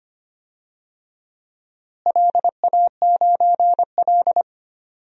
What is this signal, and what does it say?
Morse code, heard as a keyed tone, about 25 wpm, 700 Hz. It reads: LA9L
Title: DXCC-100 Call Signs - 4 Characters - Repeated 2x - 25wpm